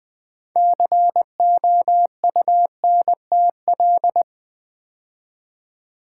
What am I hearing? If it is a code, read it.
Morse code, heard as a keyed tone, about 20 wpm, 700 Hz. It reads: COUNTL
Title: Build A Word - 3 Letters At A Time - 9-Letter Words - 20wpm